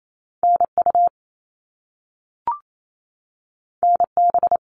Morse code, heard as a keyed tone, about 28 wpm, 700 Hz. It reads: NU NB
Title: Canadian Provinces and Territories 28wpm